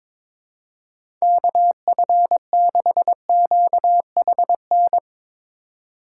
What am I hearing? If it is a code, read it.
Morse code, heard as a keyed tone, about 22 wpm, 700 Hz. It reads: KF6QHN
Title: USA Call Signs - 6 Characters - 2x Character Spacing - 22wpm